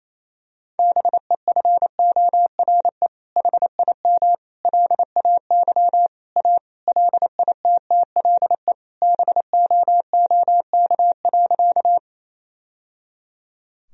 Morse code, heard as a keyed tone, about 28 wpm, 700 Hz. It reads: BEFORE HIM LAY A LITTLE BOOK.
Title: Sentences from Top 1000 Words - Review Part 2 - 28wpm